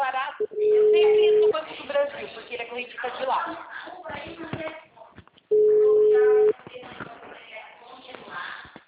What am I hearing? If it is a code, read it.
Ringback tone (Europe, 425 Hz), with speech.
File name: out- -2016- - - .4